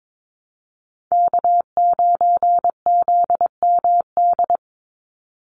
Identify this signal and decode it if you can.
Morse code, heard as a keyed tone, about 22 wpm, 700 Hz. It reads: K9ZMD
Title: NA SOTA Chaser Callsigns 22wpm